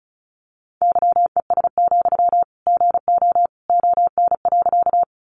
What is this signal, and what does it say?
Morse code, heard as a keyed tone, about 34 wpm, 700 Hz. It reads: YES, GO ON.
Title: Sentences from Top 500 Words 35wpm